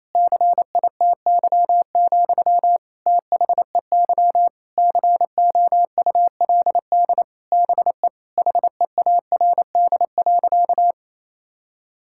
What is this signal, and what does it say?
Morse code, heard as a keyed tone, about 28 wpm, 700 Hz. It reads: CITY, THEY COULD BE HEARD.
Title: Sentences from Top 1000 Words - Review Part 5 - 28wpm